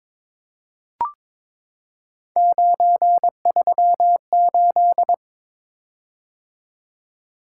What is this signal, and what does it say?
Morse code, heard as a keyed tone, about 23 wpm, 700 Hz. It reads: 938
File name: practice / 22wpm / Three Letter-Number Combinations - 22wpm